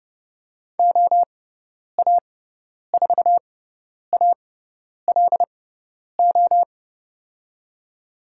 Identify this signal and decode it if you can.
Morse code, heard as a keyed tone, about 30 wpm, spaced as for 10 wpm, 700 Hz. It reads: OA4ALO
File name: Lesson 16g - Letter c - Call Signs 30wpm